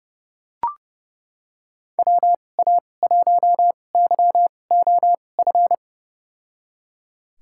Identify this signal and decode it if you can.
Morse code, heard as a keyed tone, about 32 wpm, spaced as for 22 wpm, 700 Hz. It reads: WA1YOF
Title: USA Call Signs - 6 Characters - 2x Character Spacing - 30wpm